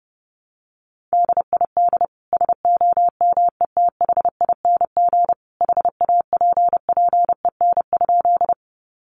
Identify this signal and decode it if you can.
Morse code, heard as a keyed tone, about 30 wpm, 700 Hz. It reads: DID SOMETHING HAPPEN?
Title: Sentences from Top 600 Words 30wpm